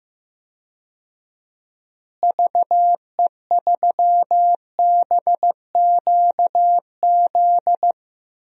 Morse code, heard as a keyed tone, about 15 wpm, 700 Hz. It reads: VE3BQZ
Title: DXCC-100 Call Signs - 6 Characters - Repeated 3x - 15wpm